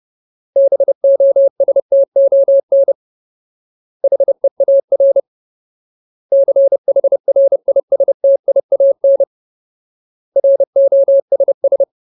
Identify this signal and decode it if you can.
Morse code, heard as a keyed tone, about 30 wpm, 550 Hz. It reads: BOSTON HEAR CHRISTIAN ROSS
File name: Ultimate Word List / 30WPM Ultimate